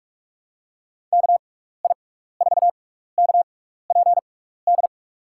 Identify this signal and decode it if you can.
Morse code, heard as a keyed tone, about 44 wpm, spaced as for 15 wpm, 700 Hz. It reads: KI4KPD
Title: USA Call Signs - 6x Character Spacing - 45wpm